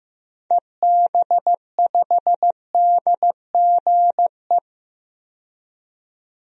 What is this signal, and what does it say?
Morse code, heard as a keyed tone, about 15 wpm, 700 Hz. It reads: EB5DGE
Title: DXCC-100 Call Signs - 6 Characters - 15wpm